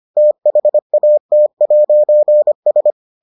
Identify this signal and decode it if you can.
Morse code, heard as a keyed tone, about 25 wpm, 600 Hz. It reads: THAT'S